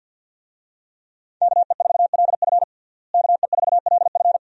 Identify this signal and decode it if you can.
Morse code, heard as a keyed tone, about 50 wpm, 700 Hz. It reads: KE4LF KE4LF
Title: USA Call Signs - 5 Characters - Repeated 2x - 50wpm